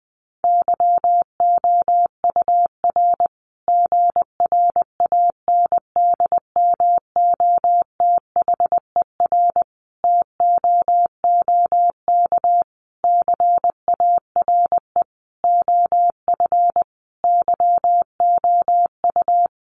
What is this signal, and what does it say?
Morse code, heard as a keyed tone, about 20 wpm, 700 Hz. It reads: YOUR GRANDMOTHER TOOK CARE OF YOU
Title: Sentences from Top 1200 Words 20wpm